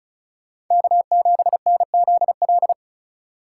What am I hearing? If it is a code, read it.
Morse code, heard as a keyed tone, about 45 wpm, 700 Hz. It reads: K7NZL